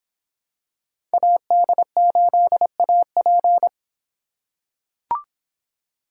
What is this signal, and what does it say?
Morse code, heard as a keyed tone, about 25 wpm, 700 Hz. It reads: AD8AP